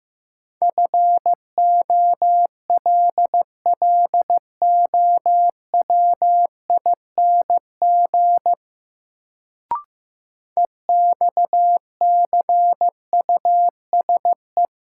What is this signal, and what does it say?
Morse code, heard as a keyed tone, about 15 wpm, 700 Hz. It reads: FOLLOWING EXCUSE
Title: Sentences from Top 700 Words - New Word Drill - 15wpm